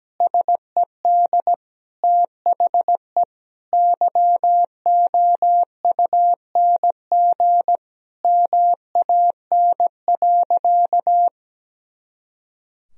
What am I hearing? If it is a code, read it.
Morse code, heard as a keyed tone, about 17 wpm, 700 Hz. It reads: SED THE YOUNG MAN.
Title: Sentences from Top 400 Words and Abbreviations 17wpm